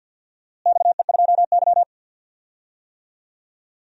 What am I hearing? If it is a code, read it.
Morse code, heard as a keyed tone, about 52 wpm, 700 Hz. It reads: XE2Y
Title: DXCC-100 Call Signs - 4 Characters - Repeated 3x - 50wpm